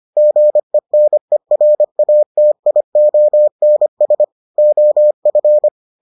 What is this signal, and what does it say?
Morse code, heard as a keyed tone, about 25 wpm, 600 Hz. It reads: GENERATIONS OF